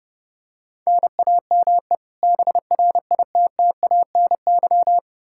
Morse code, heard as a keyed tone, about 30 wpm, 700 Hz. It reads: NAME BRITTANY
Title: K1USN SST QSO Practice 30wpm